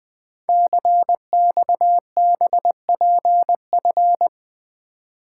Morse code, heard as a keyed tone, about 20 wpm, 700 Hz. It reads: CXBPF
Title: Code Groups - 5 Letters - 20wpm